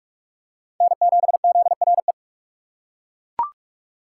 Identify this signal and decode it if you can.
Morse code, heard as a keyed tone, about 41 wpm, 700 Hz. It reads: N7ZRE
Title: USA Call Signs - 5 Characters - 2x Character Spacing - 45wpm